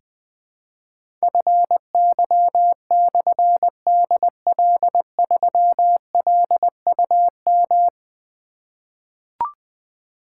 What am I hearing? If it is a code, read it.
Morse code, heard as a keyed tone, about 20 wpm, 700 Hz. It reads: FY/DL3LUM
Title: Difficult Call Signs 20wpm